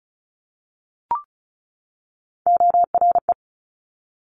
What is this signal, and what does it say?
Morse code, heard as a keyed tone, about 36 wpm, 700 Hz. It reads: ORE
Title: Top Three Letter Combinations in Words 35wpm